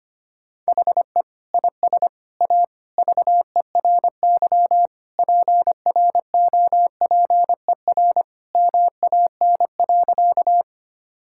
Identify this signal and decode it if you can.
Morse code, heard as a keyed tone, about 25 wpm, 700 Hz. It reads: HE IS A VERY PROPER MAN.
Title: Sentences from Top 1000 Words - Review Part 5 - 25wpm